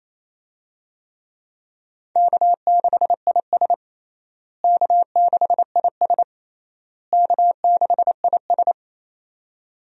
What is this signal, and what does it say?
Morse code, heard as a keyed tone, about 28 wpm, 700 Hz. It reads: K6IS K6IS K6IS
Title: USA Call Signs - 4 Characters - Repeated 3x - 28wpm